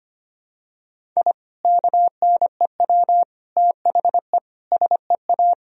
Morse code, heard as a keyed tone, about 25 wpm, 700 Hz. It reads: I KNEW THE SEA
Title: Sentences from Top 1000 Words - Review Part 1 - 25wpm